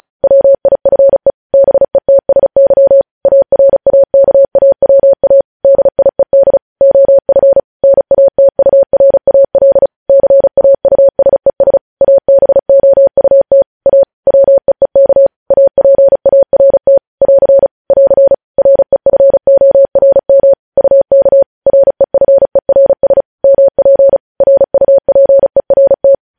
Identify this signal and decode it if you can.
Morse code, heard as a keyed tone, about 36 wpm, 550 Hz. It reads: WIFE BETSY ARAKAWA DIED OF NATURAL CAUSES ABOUT A WEEK APART <AR> <AR> REFORM UK REFERS MP RUPERT
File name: podcast / News headlines-1x-35x35